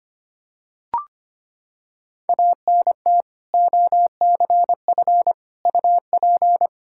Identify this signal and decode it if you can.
Morse code, heard as a keyed tone, about 25 wpm, 700 Hz. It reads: ANT OCF UP
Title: Practice QSO - Exchange 2 - Info - 25wpm